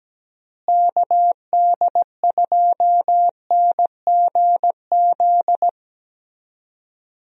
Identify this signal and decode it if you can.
Morse code, heard as a keyed tone, about 17 wpm, 700 Hz. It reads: KD2NGZ
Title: USA Call Signs - 6 Characters - Repeated 2x - 17wpm